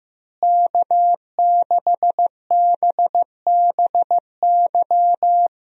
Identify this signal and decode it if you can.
Morse code, heard as a keyed tone, about 16 wpm, 700 Hz. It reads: K6BBY
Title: USA Call Signs - 5 Characters - Repeated 2x - 15wpm